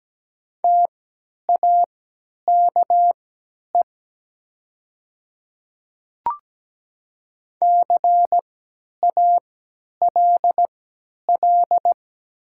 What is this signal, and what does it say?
Morse code, heard as a keyed tone, about 17 wpm, spaced as for 10 wpm, 700 Hz. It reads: TAKE CALL
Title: 3x Character Spacing - Top 300 Words - 17wpm